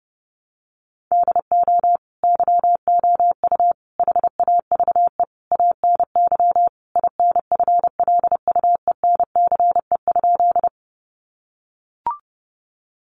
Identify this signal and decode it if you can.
Morse code, heard as a keyed tone, about 30 wpm, 700 Hz. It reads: DO YOU HAVE ANY INFLUENCE?
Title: Sentences from Top 1100 Words 30wpm